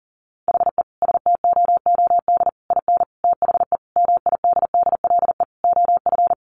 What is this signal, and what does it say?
Morse code, heard as a keyed tone, about 40 wpm, 700 Hz. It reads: HE STOOD IN THE MIDDLE OF
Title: Sentences from Top 1400 Words 40wpm